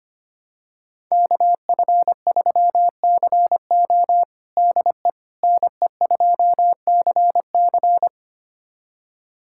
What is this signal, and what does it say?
Morse code, heard as a keyed tone, about 25 wpm, 700 Hz. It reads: KF3CO DE NE2CC
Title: USA Callsign DE Callsign 25wpm